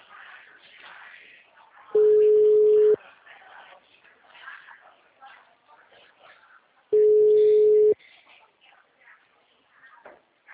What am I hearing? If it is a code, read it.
Ringback tone (Europe, 425 Hz), with speech.
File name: out- -2007